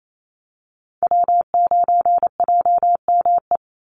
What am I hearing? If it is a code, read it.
Morse code, heard as a keyed tone, about 28 wpm, 700 Hz. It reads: W9JME
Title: USA Call Signs 28wpm